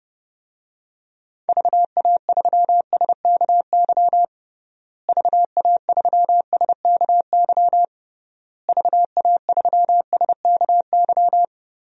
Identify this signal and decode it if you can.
Morse code, heard as a keyed tone, about 32 wpm, 700 Hz. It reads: VA3SKY VA3SKY VA3SKY
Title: DXCC-100 Call Signs - 6 Characters - Repeated 3x - 30wpm